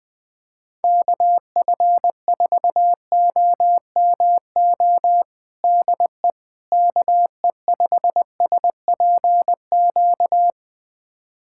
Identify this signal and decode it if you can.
Morse code, heard as a keyed tone, about 21 wpm, 700 Hz. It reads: KF4OMO DE KE5SPQ
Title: USA Callsign DE Callsign - 6 Characters - 20wpm